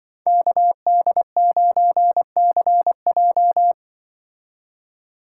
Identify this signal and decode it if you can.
Morse code, heard as a keyed tone, about 24 wpm, 700 Hz. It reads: KD9CJ
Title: Speed Racing - USA Call Signs 20wpm